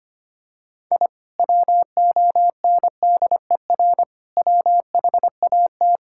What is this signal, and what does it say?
Morse code, heard as a keyed tone, about 25 wpm, 700 Hz. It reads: I WONDER WHAT
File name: practice / 25wpm / Sentences with 2-Word Phrases - 25wpm